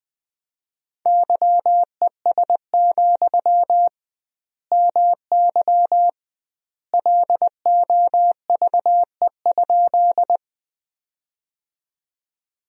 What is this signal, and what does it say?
Morse code, heard as a keyed tone, about 21 wpm, 700 Hz. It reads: YES, MY LOVE?
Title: 2x Word-Spacing - Sentences from Top 500 Words - 20wpm